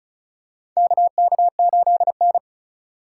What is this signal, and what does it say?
Morse code, heard as a keyed tone, about 36 wpm, 700 Hz. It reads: KK8N
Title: USA Call Signs - 4 Characters - 4x Character Spacing - 35wpm